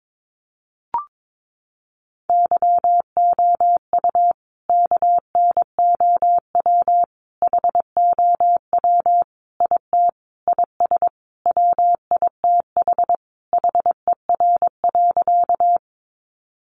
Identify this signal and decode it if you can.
Morse code, heard as a keyed tone, about 22 wpm, 700 Hz. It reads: YOU KNOW HOW IT IS WITH HER.
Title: Sentences from Top 100 Words 22wpm